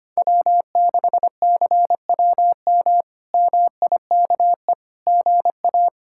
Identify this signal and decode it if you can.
Morse code, heard as a keyed tone, about 25 wpm, 700 Hz. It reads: W6CWM MIKE GA
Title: K1USN SST QSO Practice - 25wpm